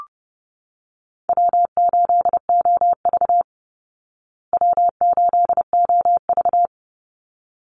Morse code, heard as a keyed tone, about 30 wpm, 700 Hz. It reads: W8OV W8OV
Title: Easy Call Signs Repeated 2x - 30wpm